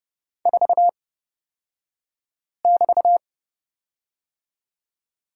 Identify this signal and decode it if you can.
Morse code, heard as a keyed tone, about 30 wpm, spaced as for 5 wpm, 700 Hz. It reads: 4<BT>
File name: Lesson 37d - Letter x - Two Characters 30wpm